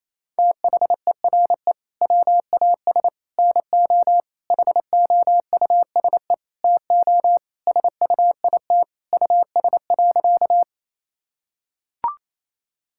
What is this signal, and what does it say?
Morse code, heard as a keyed tone, about 28 wpm, 700 Hz. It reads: THERE WAS NO HOUSE TO SUIT US.
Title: Sentences from Top 1600 Words 28wpm